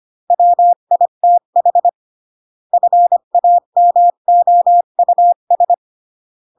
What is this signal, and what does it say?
Morse code, heard as a keyed tone, about 25 wpm, 700 Hz. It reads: WITH FAMOUS